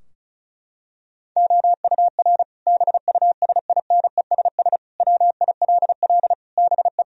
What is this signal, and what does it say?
Morse code, heard as a keyed tone, about 34 wpm, 700 Hz. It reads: OUR BUSINESS WILL BE
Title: Sentences from Top 1000 Words - Review Part 4 - 35wpm